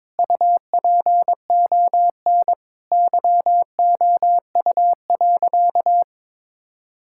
Morse code, heard as a keyed tone, about 22 wpm, 700 Hz. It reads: UPON YOU.
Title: Sentences from Top 1600 Words 22wpm